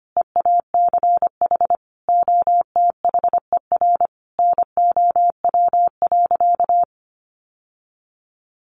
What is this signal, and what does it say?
Morse code, heard as a keyed tone, about 25 wpm, 700 Hz. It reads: EACH OTHER NOW.
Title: Sentences from Top 1200 Words 25wpm